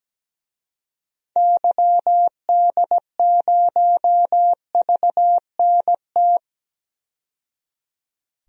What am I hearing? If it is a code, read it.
Morse code, heard as a keyed tone, about 17 wpm, 700 Hz. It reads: YD0VNT
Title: DXCC-100 Call Signs - 6 Characters - 17wpm